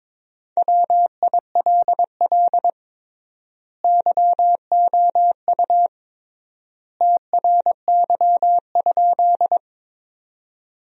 Morse code, heard as a keyed tone, about 22 wpm, 700 Hz. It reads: WILL YOU TRY?
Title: 3x Word-Spacing - Sentences from Top 200 Words - 22wpm